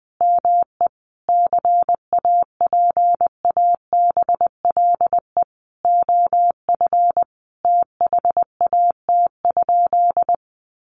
Morse code, heard as a keyed tone, about 20 wpm, 700 Hz. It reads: ME CAPABLE OF THAT?
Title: Sentences from Top 1200 Words 20wpm